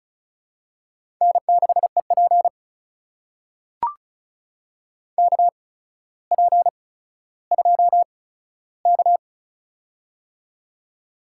Morse code, heard as a keyed tone, about 36 wpm, spaced as for 15 wpm, 700 Hz. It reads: N6EP KP2K
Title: USA Call Signs - 4 Characters - 8x Character Spacing - 35wpm